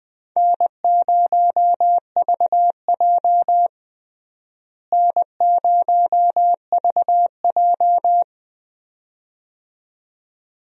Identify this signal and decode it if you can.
Morse code, heard as a keyed tone, about 21 wpm, 700 Hz. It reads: N0VJ N0VJ
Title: USA Call Signs - 4 Characters - Repeated 2x - 20wpm